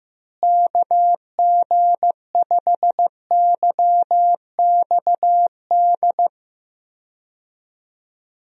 Morse code, heard as a keyed tone, about 15 wpm, 700 Hz. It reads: KG5YXD
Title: USA Call Signs - 6 Characters - Repeated 3x - 15wpm